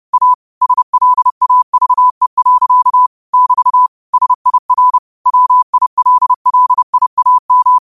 Morse code, heard as a keyed tone, about 30 wpm, 1000 Hz. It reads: A SLAVE. <BT> SIR WILLIAM